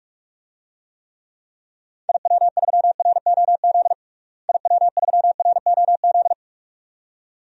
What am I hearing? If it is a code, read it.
Morse code, heard as a keyed tone, about 46 wpm, 700 Hz. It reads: IW3ROZ IW3ROZ
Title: DXCC-100 Call Signs - 6 Characters - Repeated 2x - 45wpm